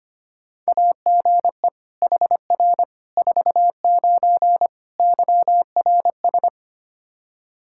Morse code, heard as a keyed tone, about 25 wpm, 700 Hz. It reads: AGE HR 49 YRS
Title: Practice QSO - Exchange 3 - More Info - 25wpm